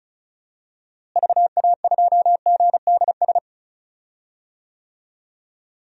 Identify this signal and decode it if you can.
Morse code, heard as a keyed tone, about 36 wpm, 700 Hz. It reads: VA2GDS
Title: DXCC-100 Call Signs - 6 Characters - Repeated 2x - 35wpm